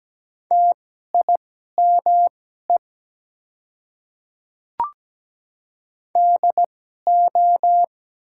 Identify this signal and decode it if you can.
Morse code, heard as a keyed tone, about 17 wpm, spaced as for 12 wpm, 690 Hz. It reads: TIME DO
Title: 2x Character Spacing - Top 100 Words - 17wpm